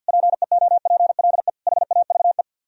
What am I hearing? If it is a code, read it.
Morse code, heard as a keyed tone, about 64 wpm, 700 Hz. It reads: PEOPLE HAVE